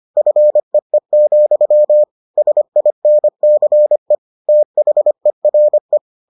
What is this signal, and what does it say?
Morse code, heard as a keyed tone, about 25 wpm, 600 Hz. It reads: FEE, SINCE THERE